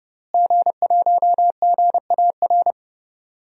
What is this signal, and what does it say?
Morse code, heard as a keyed tone, about 30 wpm, 700 Hz. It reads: G1GAR
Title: DXCC-100 Call Signs - 5 Characters - Repeated 2x - 30wpm